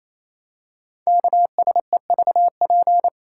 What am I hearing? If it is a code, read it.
Morse code, heard as a keyed tone, about 27 wpm, 700 Hz. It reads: KSEVP
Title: Code Groups - 5 Letter:Numbers - 28wpm